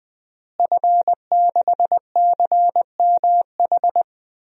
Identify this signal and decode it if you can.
Morse code, heard as a keyed tone, about 20 wpm, 700 Hz. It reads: F6CMH
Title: DXCC-100 Call Signs - 5 Characters - Repeated 3x - 20wpm